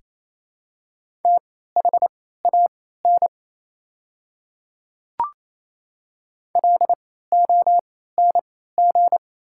Morse code, heard as a keyed tone, about 28 wpm, spaced as for 16 wpm, 690 Hz. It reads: THAN LONG
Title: 3x Character Spacing - Top 100 Words - 28wpm